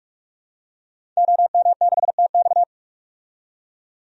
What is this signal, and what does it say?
Morse code, heard as a keyed tone, about 42 wpm, 700 Hz. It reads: OM6TX
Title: DXCC-100 Call Signs - 5 Characters - 45wpm